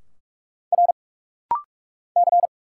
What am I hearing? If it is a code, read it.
Morse code, heard as a keyed tone, about 44 wpm, 700 Hz. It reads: R C
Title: Single Letter-Number - Rapid-Fire 45wpm